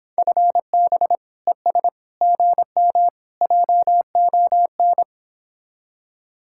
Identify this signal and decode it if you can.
Morse code, heard as a keyed tone, about 26 wpm, 700 Hz. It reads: FB ES GM JON